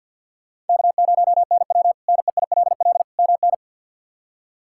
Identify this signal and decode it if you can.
Morse code, heard as a keyed tone, about 52 wpm, 700 Hz. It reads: K0NW NEILL MN
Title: K1USN SST QSO Practice - 50wpm